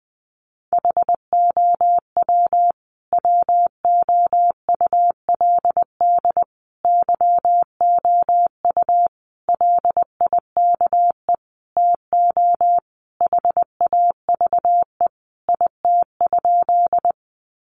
Morse code, heard as a keyed tone, about 20 wpm, 700 Hz. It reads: HOW WOULD YOU LIKE TO HAVE IT?
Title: Sentences from Top 100 Words 20wpm